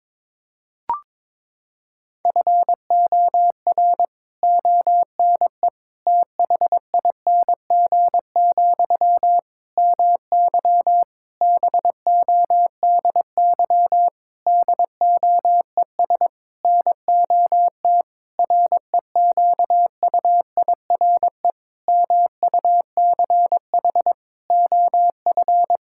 Morse code, heard as a keyed tone, about 22 wpm, 700 Hz. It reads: FOR ONE THING, MY BODY DOES NOT REQUIRE MUCH OF
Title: Sentences from Top 1000 Words - Review Part 6 - 22wpm